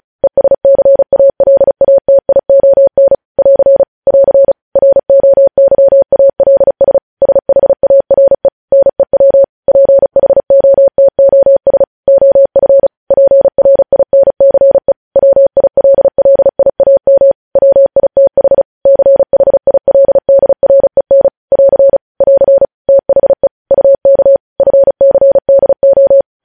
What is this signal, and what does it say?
Morse code, heard as a keyed tone, about 36 wpm, 550 Hz. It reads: ESCALATION <AR> <AR> ROYALS SHARE NEW PHOTOS OF PRINCE WILLIAM WITH CHILDREN <AR> <AR> THE UK FCDO
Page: news headlines in morse code at 35 wpm